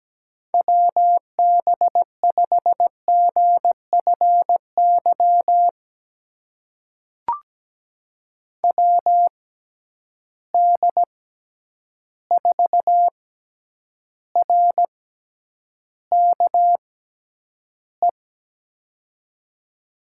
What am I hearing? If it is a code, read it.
Morse code, heard as a keyed tone, about 17 wpm, spaced as for 9 wpm, 700 Hz. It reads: WB5GFY WD4RKE
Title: USA Call Signs - 6 Characters - 6x Character Spacing - 17wpm